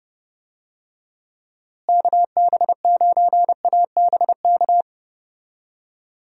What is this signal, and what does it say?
Morse code, heard as a keyed tone, about 30 wpm, 700 Hz. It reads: KB9ABK
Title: USA Call Signs - 6 Characters - Repeated 3x - 30wpm